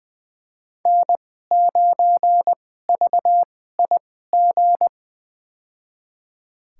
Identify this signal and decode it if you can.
Morse code, heard as a keyed tone, about 20 wpm, spaced as for 14 wpm, 700 Hz. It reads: N9VIG
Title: USA Call Signs - 5 Characters - 2x Character Spacing - 20wpm